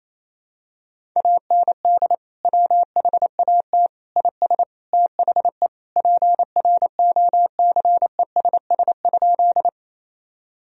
Morse code, heard as a keyed tone, about 28 wpm, 700 Hz. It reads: AND WHAT IS THE PROCESS?
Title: Sentences from Top 1000 Words - Review Part 3 - 28wpm